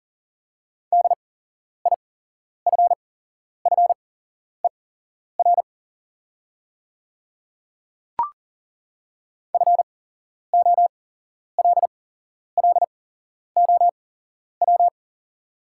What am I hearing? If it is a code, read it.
Morse code, heard as a keyed tone, about 39 wpm, spaced as for 11 wpm, 700 Hz. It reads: DIFFER FOLLOW
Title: 8x Character Spacing - Top 200 Words - 40wpm